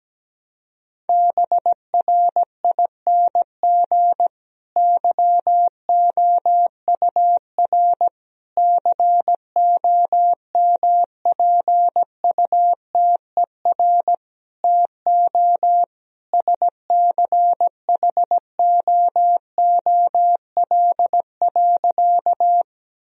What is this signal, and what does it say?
Morse code, heard as a keyed tone, about 17 wpm, 700 Hz. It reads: BRING YOUR COMPUTER TO SCHOOL.
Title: Sentences from Top 1000 Words - Review Part 3 - 17wpm